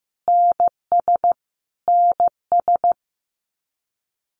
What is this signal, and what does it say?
Morse code, heard as a keyed tone, about 15 wpm, 700 Hz. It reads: NS NS
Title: Canadian Provinces and Territories - Repeated 3x - 15wpm